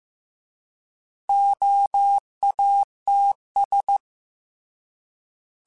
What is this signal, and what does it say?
Morse code, heard as a keyed tone, about 15 wpm, 800 Hz. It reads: OATS